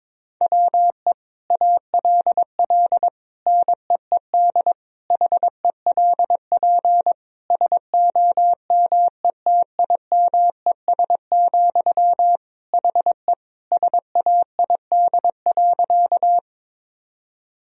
Morse code, heard as a keyed tone, about 22 wpm, 700 Hz. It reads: WE ALL NEED HELP SOMETIMES, HE SAID.
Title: Sentences from Top 1000 Words - Review Part 3 - 22wpm